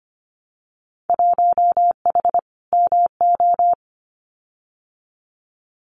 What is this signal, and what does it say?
Morse code, heard as a keyed tone, about 25 wpm, 700 Hz. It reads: 1H MO
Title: Winter Field Day Exchanges 25wpm